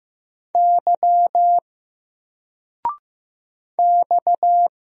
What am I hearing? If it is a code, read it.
Morse code, heard as a keyed tone, about 15 wpm, 700 Hz. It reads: Y X
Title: Single Letter-Number 15wpm